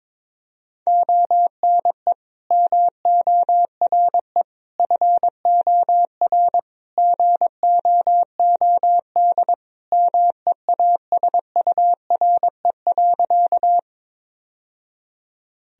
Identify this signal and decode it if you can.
Morse code, heard as a keyed tone, about 22 wpm, 700 Hz. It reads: ONE MORE FOR GOOD MEASURE.
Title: Sentences from Top 1000 Words - Review Part 2 - 22wpm